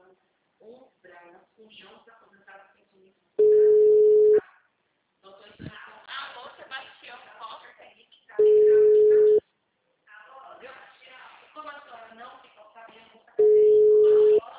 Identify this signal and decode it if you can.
Ringback tone (Europe, 425 Hz), with speech.
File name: out- -2027- - - .56222